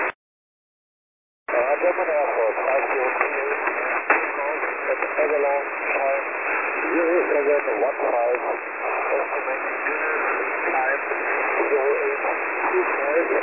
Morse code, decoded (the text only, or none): none